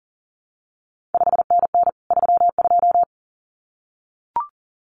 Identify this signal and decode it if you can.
Morse code, heard as a keyed tone, about 38 wpm, 700 Hz. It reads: 5NN 32
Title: CQ WW DX Contest Exchange Encore - 40wpm